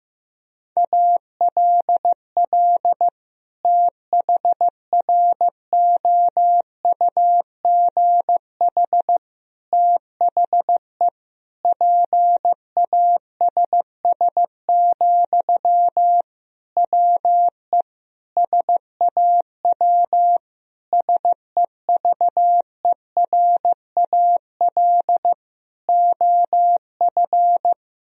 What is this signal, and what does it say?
Morse code, heard as a keyed tone, about 15 wpm, 700 Hz. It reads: ALL THROUGH THE PASS, WE SAW SEVERAL OF
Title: Sentences from Top 1000 Words - Review Part 2 - 15wpm